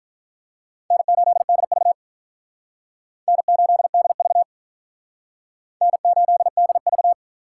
Morse code, heard as a keyed tone, about 48 wpm, 700 Hz. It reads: N8DV N8DV N8DV